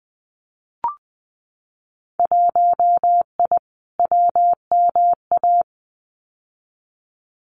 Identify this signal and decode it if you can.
Morse code, heard as a keyed tone, about 20 wpm, 700 Hz. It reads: 1I WMA
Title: Winter Field Day Exchanges Encore 20wpm